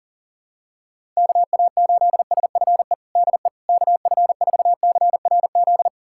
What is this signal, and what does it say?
Morse code, heard as a keyed tone, about 40 wpm, 700 Hz. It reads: KA8SFE DE KF4CRZ
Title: USA Callsign DE Callsign - 6 Characters - 40wpm